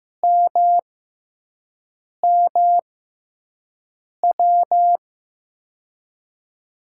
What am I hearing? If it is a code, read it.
Morse code, heard as a keyed tone, about 15 wpm, spaced as for 5 wpm, 700 Hz. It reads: MMW
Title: USA Call Signs - 6 Characters - 6x Character Spacing - 15wpm